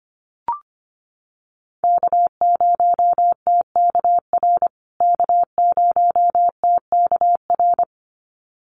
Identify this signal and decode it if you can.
Morse code, heard as a keyed tone, about 25 wpm, 700 Hz. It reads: K0TKR K0TKR
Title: USA Call Signs Repeated 2x 25wpm